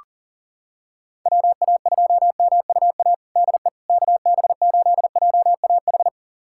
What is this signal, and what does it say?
Morse code, heard as a keyed tone, about 37 wpm, 700 Hz. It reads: WA2MUA DE KB8JAH
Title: USA Callsign DE Callsign 40wpm